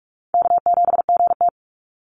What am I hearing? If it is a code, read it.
Morse code, heard as a keyed tone, about 46 wpm, 690 Hz. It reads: K7GT
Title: NA SOTA Chaser Callsigns 45wpm